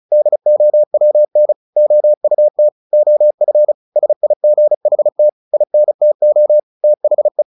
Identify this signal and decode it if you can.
Morse code, heard as a keyed tone, about 35 wpm, 600 Hz. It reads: DOWN OUT OF SIGHT INTO THE
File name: the old man and the sea / the old man and the sea 1 35wpm